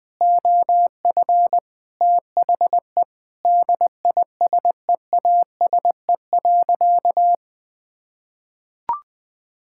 Morse code, heard as a keyed tone, about 20 wpm, 700 Hz. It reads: OF THE DISEASE.
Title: Sentences from Top 1000 Words - Review Part 6 - 20wpm